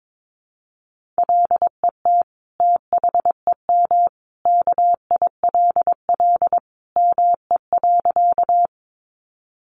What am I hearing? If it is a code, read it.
Morse code, heard as a keyed tone, about 22 wpm, 700 Hz. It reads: LET THEM KILL ME.
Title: Sentences from Top 1400 Words 22wpm